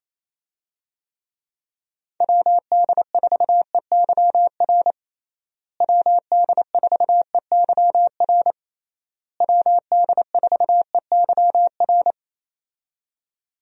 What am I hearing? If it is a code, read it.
Morse code, heard as a keyed tone, about 27 wpm, 700 Hz. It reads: WD4EYR WD4EYR WD4EYR
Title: USA Call Signs - 6 Characters - Repeated 3x - 28wpm